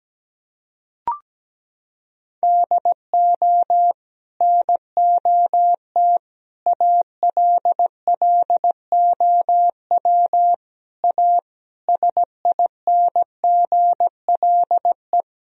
Morse code, heard as a keyed tone, about 17 wpm, 700 Hz. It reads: DO NOT ALLOW A SINGLE
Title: Sentences from Top 1000 Words - Review Part 4 - 17wpm